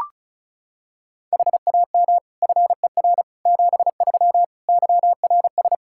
Morse code, heard as a keyed tone, about 36 wpm, 700 Hz. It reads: HAM FER 73 YRS
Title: Practice QSO - Exchange 3 - More Info - 35wpm